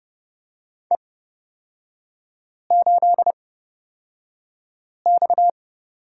Morse code, heard as a keyed tone, about 30 wpm, spaced as for 5 wpm, 700 Hz. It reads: E8X
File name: Lesson 37e - Letter x - Three Characters 30wpm